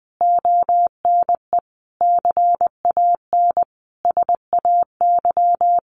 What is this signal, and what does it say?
Morse code, heard as a keyed tone, about 20 wpm, 700 Hz. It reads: ONE CAN SAY
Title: Sentences from Top 200 Words 20wpm